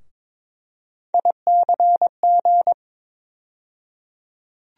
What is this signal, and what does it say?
Morse code, heard as a keyed tone, about 22 wpm, 700 Hz. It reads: ICG
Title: USA Call Sign Postfixes - 3 Letters 22wpm